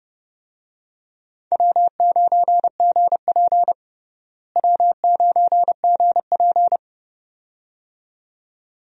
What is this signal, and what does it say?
Morse code, heard as a keyed tone, about 32 wpm, 700 Hz. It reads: W9GP W9GP
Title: USA Call Signs - 4 Characters - Repeated 2x - 30wpm